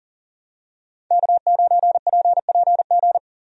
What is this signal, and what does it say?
Morse code, heard as a keyed tone, about 37 wpm, 700 Hz. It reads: K9PPG
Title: USA Call Signs - 5 Characters - 40wpm